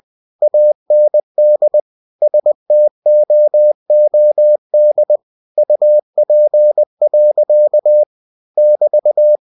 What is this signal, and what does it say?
Morse code, heard as a keyed tone, about 20 wpm, 600 Hz. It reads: AND STOOD UP. <BT>
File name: cw-books / winnie-the-pooh / winnie-20wpm-page-0025